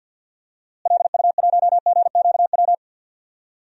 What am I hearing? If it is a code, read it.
Morse code, heard as a keyed tone, about 46 wpm, 700 Hz. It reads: LU1GQW
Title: DXCC-100 Call Signs - 6 Characters - Repeated 2x - 50wpm